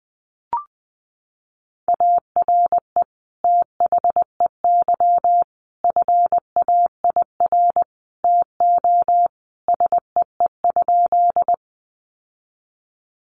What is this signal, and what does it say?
Morse code, heard as a keyed tone, about 20 wpm, 700 Hz. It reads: ARE THEY FAIR TO SEE?
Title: Sentences from Top 1300 Words 20wpm